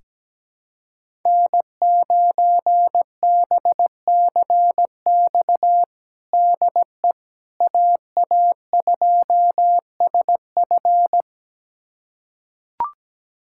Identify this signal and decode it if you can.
Morse code, heard as a keyed tone, about 17 wpm, 700 Hz. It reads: N9BCX DE AA2SF
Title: USA Callsign DE Callsign - 5 Characters - 17wpm